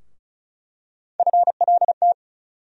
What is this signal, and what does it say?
Morse code, heard as a keyed tone, about 36 wpm, 700 Hz. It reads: FLT
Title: USA Call Sign Postfixes - 3 Letters 35wpm